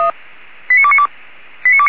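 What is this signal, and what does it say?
Five-tone selective call (ZVEI), heard as quick runs of five tones, tones of about 70 ms. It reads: 89292
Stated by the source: ZVEI five-tone selective call received on an SDR.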